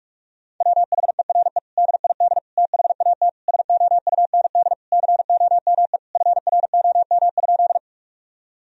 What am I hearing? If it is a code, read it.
Morse code, heard as a keyed tone, about 46 wpm, 700 Hz. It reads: WHERE DID THAT SOUND COME FROM?
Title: Sentences from Top 1000 Words - Review Part 1 - 45wpm